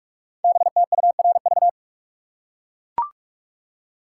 Morse code, heard as a keyed tone, about 44 wpm, 700 Hz. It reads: BTURV
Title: Code Groups - 5 Letters - 45wpm